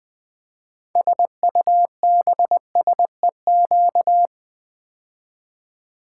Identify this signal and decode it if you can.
Morse code, heard as a keyed tone, about 20 wpm, 700 Hz. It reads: SUBSEQ
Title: Build A Word - 6 Letters At A Time - 12-Letter Words - 20wpm